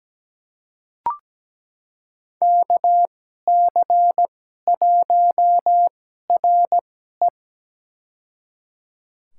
Morse code, heard as a keyed tone, about 17 wpm, spaced as for 12 wpm, 700 Hz. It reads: KC1RE
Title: USA Call Signs - 2x Character Spacing - 17wpm